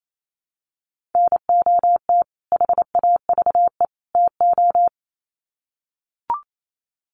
Morse code, heard as a keyed tone, about 28 wpm, 700 Hz. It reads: NOT HAVE TO
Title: Sets of 3 Words 28wpm